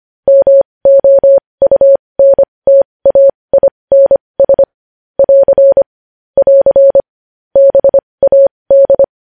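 Morse code, heard as a keyed tone, about 25 wpm, spaced as for 20 wpm, 550 Hz. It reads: MOUNTAINS <AR> <AR> BAD
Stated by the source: Morse news headlines sent at 20, opening with vvv.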